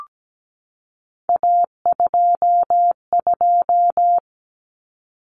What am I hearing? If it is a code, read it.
Morse code, heard as a keyed tone, about 17 wpm, 700 Hz. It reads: A22
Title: Common Cut Numbers - 3-Digit 17wpm